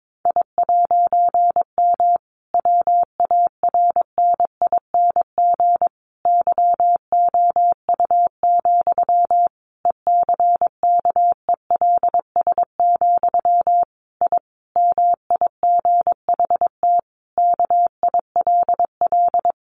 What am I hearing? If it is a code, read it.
Morse code, heard as a keyed tone, about 22 wpm, 700 Hz. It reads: I'M WARNING YOU, ECKELS, I MIGHT KILL
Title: A Sound Of Thunder 22wpm